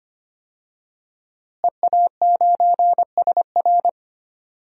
Morse code, heard as a keyed tone, about 25 wpm, 700 Hz. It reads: EA9SR
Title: DXCC-100 Call Signs - 5 Characters - Repeated 2x - 25wpm